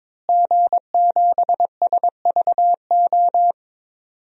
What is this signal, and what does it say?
Morse code, heard as a keyed tone, about 22 wpm, 700 Hz. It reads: G7SVO
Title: DXCC-100 Call Signs - 5 Characters - Repeated 3x - 22wpm